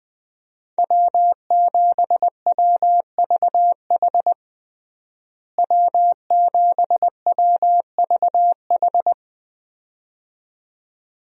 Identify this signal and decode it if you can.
Morse code, heard as a keyed tone, about 20 wpm, 700 Hz. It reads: W7WVH W7WVH
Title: USA Call Signs - 5 Characters - Repeated 2x - 20wpm